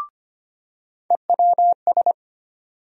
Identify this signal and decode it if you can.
Morse code, heard as a keyed tone, about 25 wpm, 700 Hz. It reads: EWS
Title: USA Call Sign Postfixes - 3 Letters 25wpm